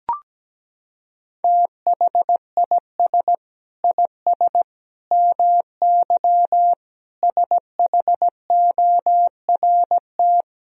Morse code, heard as a keyed tone, about 17 wpm, 700 Hz. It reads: THIS IS MY SHORT
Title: Sentences from Top 1000 Words - Review Part 4 - 17wpm